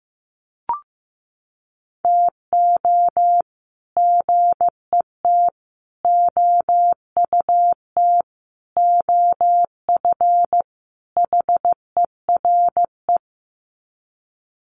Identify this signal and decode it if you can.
Morse code, heard as a keyed tone, about 15 wpm, 700 Hz. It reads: TO GET OUT OF HERE
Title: Sets of 5 Words Encore 15wpm